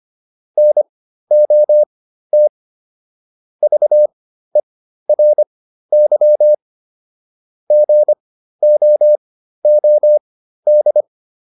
Morse code, heard as a keyed tone, about 25 wpm, spaced as for 13 wpm, 600 Hz. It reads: NOT VERY GOOD
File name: QSO 407 13